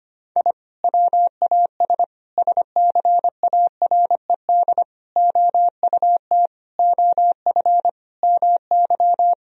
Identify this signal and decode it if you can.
Morse code, heard as a keyed tone, about 25 wpm, 700 Hz. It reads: I WAS SCARED OUT OF MY
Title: Sentences from Top 1000 Words - Review Part 4 - 25wpm